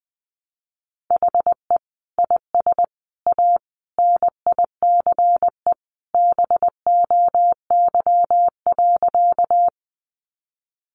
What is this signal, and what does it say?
Morse code, heard as a keyed tone, about 20 wpm, 700 Hz. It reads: HE IS A NICE BOY.
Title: Sentences from Top 1100 Words 20wpm